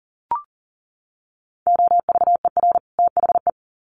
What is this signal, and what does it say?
Morse code, heard as a keyed tone, about 40 wpm, 700 Hz. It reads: OVER THE
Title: Sets of 2 Words 40wpm